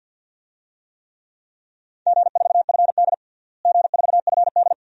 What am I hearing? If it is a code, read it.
Morse code, heard as a keyed tone, about 46 wpm, 700 Hz. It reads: G4FD G4FD
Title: DXCC-100 Call Signs - 4 Characters - Repeated 2x - 50wpm